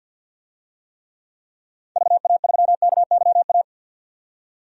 Morse code, heard as a keyed tone, about 47 wpm, 700 Hz. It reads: VA3KYA
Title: DXCC-100 Call Signs - 6 Characters - 50wpm